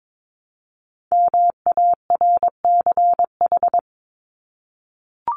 Morse code, heard as a keyed tone, about 22 wpm, 700 Hz. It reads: MARCH
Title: Top 1501-1750 Words 22wpm